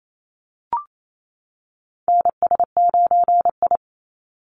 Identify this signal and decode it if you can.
Morse code, heard as a keyed tone, about 28 wpm, 700 Hz. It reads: NS9I
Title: Call Signs - 28wpm